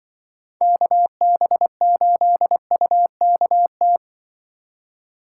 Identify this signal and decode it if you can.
Morse code, heard as a keyed tone, about 24 wpm, 700 Hz. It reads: KB8UKT
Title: Speed Racing - USA 2x3 Call Signs 20wpm